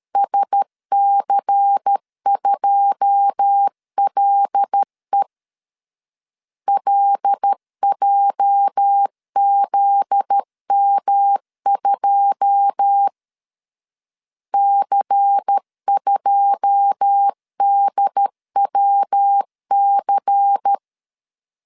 Morse code, heard as a keyed tone, about 13 wpm, 800 Hz. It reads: SC2LE LJZM2 C2DWC
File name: LEZIONI / 9 2 12wpm 2301